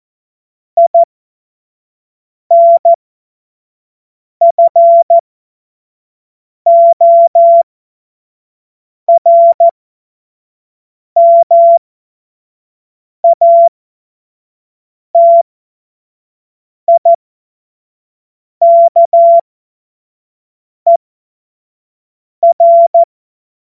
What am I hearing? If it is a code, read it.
Morse code, heard as a keyed tone, about 14 wpm, spaced as for 5 wpm, 680 Hz. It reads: INFORMATIKER
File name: Morsekurs / Witze / Witz 14. Dezember 2021, 25 F70